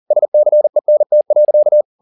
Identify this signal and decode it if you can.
Morse code, heard as a keyed tone, about 40 wpm, 600 Hz. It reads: SCENT.
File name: the old man and the sea 2 40wpm